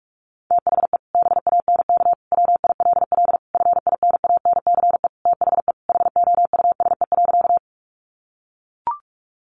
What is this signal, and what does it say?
Morse code, heard as a keyed tone, about 44 wpm, 700 Hz. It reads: THE BANK WILL FINANCE THE HOUSE.
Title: Sentences from Top 1500 Words 45wpm